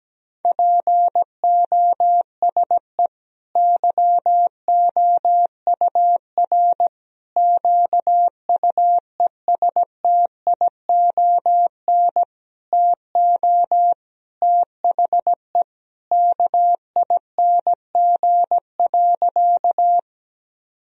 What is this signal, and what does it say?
Morse code, heard as a keyed tone, about 17 wpm, 700 Hz. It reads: POSE YOUR QUESTION TO THE KING.
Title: Sentences from Top 1000 Words - Review Part 2 - 17wpm